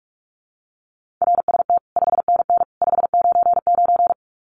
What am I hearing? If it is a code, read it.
Morse code, heard as a keyed tone, about 44 wpm, 700 Hz. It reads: RST 5NN 599
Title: Common QSO Phrases 45wpm